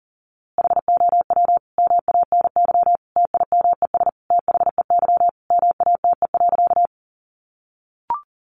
Morse code, heard as a keyed tone, about 40 wpm, 700 Hz. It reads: HOW MANY TIMES THEY MATE.
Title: A Sound Of Thunder 40wpm